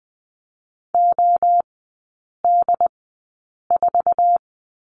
Morse code, heard as a keyed tone, about 20 wpm, spaced as for 8 wpm, 700 Hz. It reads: OD4
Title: 074 Lesson 13e - Letter L - Three Characters 20wpm